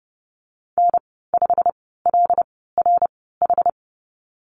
Morse code, heard as a keyed tone, about 30 wpm, spaced as for 17 wpm, 700 Hz. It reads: N5LRH
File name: Lesson 15g - Number 5 - Call Signs 30wpm